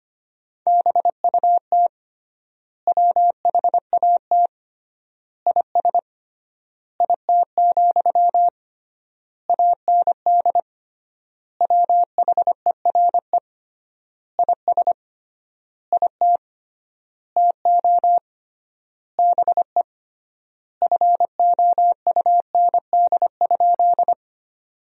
Morse code, heard as a keyed tone, about 25 wpm, 700 Hz. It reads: BUT WHAT IS IT, AND WHERE IS IT TO BE FOUND?
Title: 3x Word-Spacing - Sentences from Top 300 Words - 25wpm